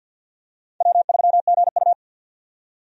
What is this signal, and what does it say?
Morse code, heard as a keyed tone, about 53 wpm, 700 Hz. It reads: W3GU
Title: USA Call Signs - 4 Characters - 50wpm